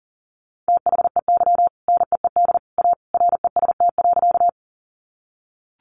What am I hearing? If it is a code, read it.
Morse code, heard as a keyed tone, about 40 wpm, 700 Hz. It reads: THEY NEED A REST.
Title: Sentences from Top 500 Words 40wpm